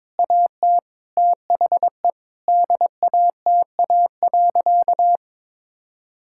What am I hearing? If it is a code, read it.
Morse code, heard as a keyed tone, about 22 wpm, 700 Hz. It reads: AT THE DATA.
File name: Sentences with 7-Word Phrases - 22wpm